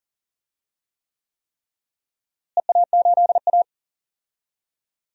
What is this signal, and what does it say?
Morse code, heard as a keyed tone, about 40 wpm, 700 Hz. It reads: EA8A
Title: DXCC-100 Call Signs - 4 Characters - Repeated 3x - 40wpm